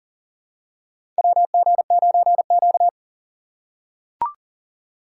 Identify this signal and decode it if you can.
Morse code, heard as a keyed tone, about 40 wpm, 700 Hz. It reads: WG9Q
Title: USA Call Signs - 4 Characters - 4x Character Spacing - 40wpm